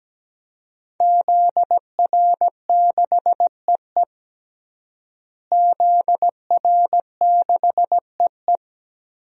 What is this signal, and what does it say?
Morse code, heard as a keyed tone, about 17 wpm, 700 Hz. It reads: ZR6EE ZR6EE
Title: DXCC-100 Call Signs - 5 Characters - Repeated 2x - 17wpm